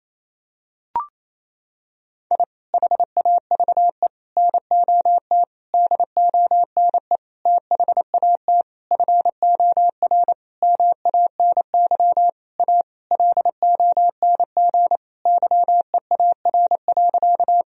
Morse code, heard as a keyed tone, about 28 wpm, 700 Hz. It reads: I HAVE NOT DONE THAT FOR MANY A LONG YEAR.
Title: Sentences from Top 1000 Words - Review Part 1 - 28wpm